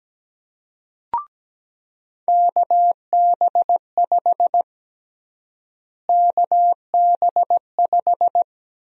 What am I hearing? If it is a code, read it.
Morse code, heard as a keyed tone, about 17 wpm, 700 Hz. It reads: KB5 KB5
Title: USA Call Sign Prefixes - Repeated 2x - 17wpm